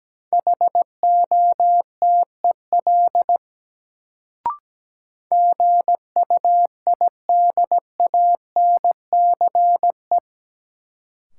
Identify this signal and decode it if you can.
Morse code, heard as a keyed tone, about 17 wpm, 700 Hz. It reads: HOTEL GUIDANCE
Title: Sentences from Top 1600 Words - New Word Drill - 17wpm